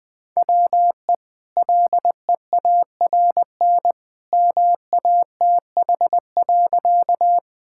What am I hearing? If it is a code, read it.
Morse code, heard as a keyed tone, about 20 wpm, 700 Hz. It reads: WE LEARN MATH.
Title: Easy Sentences with Hints - 20wpm